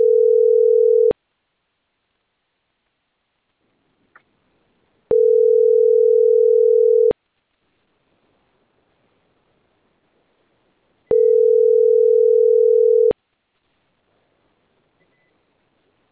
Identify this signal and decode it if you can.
Ringback tone (North America), 440 + 480 Hz.